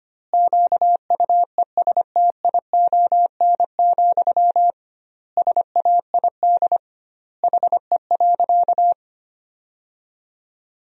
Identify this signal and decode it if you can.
Morse code, heard as a keyed tone, about 25 wpm, 700 Hz. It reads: QUESTION, SAID HE.
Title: 2x Word-Spacing - Sentences from Top 400 Words - 25wpm